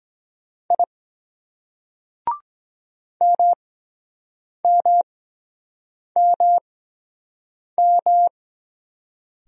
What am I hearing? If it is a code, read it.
Morse code, heard as a keyed tone, about 23 wpm, spaced as for 7 wpm, 700 Hz. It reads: I MMMM